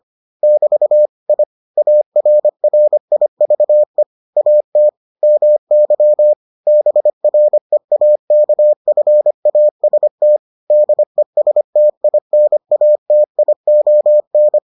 Morse code, heard as a keyed tone, about 25 wpm, 600 Hz. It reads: <BT> I ARRIVE AT MY BREAKFAST DESTINATION